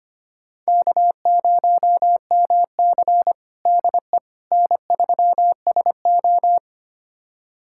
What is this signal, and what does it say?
Morse code, heard as a keyed tone, about 25 wpm, 700 Hz. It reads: K0MC DE N3SO
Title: USA Callsign DE Callsign - 4 Characters - 25wpm